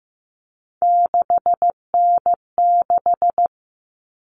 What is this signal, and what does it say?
Morse code, heard as a keyed tone, about 15 wpm, 700 Hz. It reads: 6N6
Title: All Cut Numbers - 3-Digit 15wpm